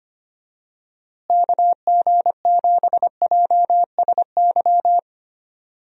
Morse code, heard as a keyed tone, about 25 wpm, 700 Hz. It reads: KG7JSY
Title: USA Call Signs - 6 Characters - Repeated 2x - 25wpm